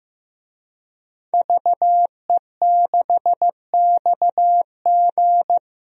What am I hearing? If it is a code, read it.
Morse code, heard as a keyed tone, about 15 wpm, 700 Hz. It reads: VE6XG
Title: DXCC-100 Call Signs - 5 Characters - Repeated 2x - 15wpm